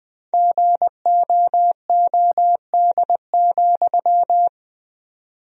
Morse code, heard as a keyed tone, about 20 wpm, 700 Hz. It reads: GOOD,
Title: 4x Word-Spacing - Sentences from Top 200 Words - 20wpm